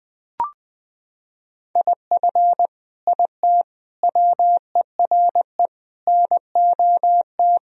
Morse code, heard as a keyed tone, about 20 wpm, 700 Hz. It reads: IF IT WERE NOT
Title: Sentences from Top 1000 Words - Review Part 2 - 20wpm